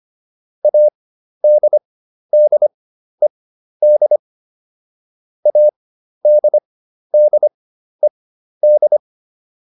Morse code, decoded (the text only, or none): ADDED ADDED